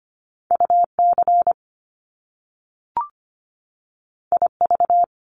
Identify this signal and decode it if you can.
Morse code, heard as a keyed tone, about 25 wpm, 700 Hz. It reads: UC IV
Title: Top Two Letter Combinations in Words 25wpm